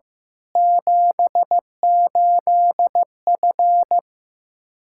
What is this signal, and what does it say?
Morse code, heard as a keyed tone, about 15 wpm, 700 Hz. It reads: 78F
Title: Practice QSO - Exchange 2 - Info - 15wpm